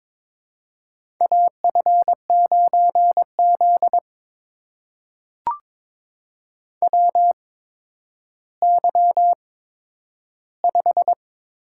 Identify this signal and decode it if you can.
Morse code, heard as a keyed tone, about 22 wpm, spaced as for 11 wpm, 700 Hz. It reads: AF9Z WY5
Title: USA Call Signs - 4 Characters - 8x Character Spacing - 22wpm